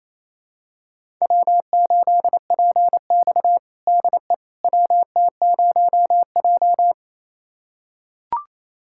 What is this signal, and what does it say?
Morse code, heard as a keyed tone, about 28 wpm, 700 Hz. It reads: W8PX DE WT0J
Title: USA Callsign DE Callsign - 4 Characters - 28wpm